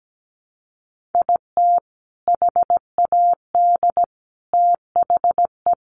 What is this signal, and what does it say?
Morse code, heard as a keyed tone, about 17 wpm, 700 Hz. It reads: IT HAD THE
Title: Sentences from Top 1300 Words 17wpm